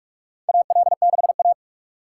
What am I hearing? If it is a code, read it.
Morse code, heard as a keyed tone, about 47 wpm, 700 Hz. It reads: AL6A
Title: USA Call Signs - 4x Character Spacing - 45wpm